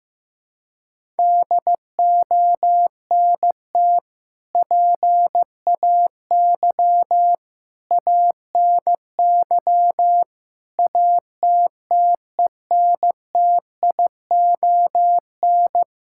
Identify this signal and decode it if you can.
Morse code, heard as a keyed tone, about 15 wpm, 700 Hz. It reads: DONT PAY ANY ATTENTION
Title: Sentences from Top 1000 Words - Review Part 6 - 15wpm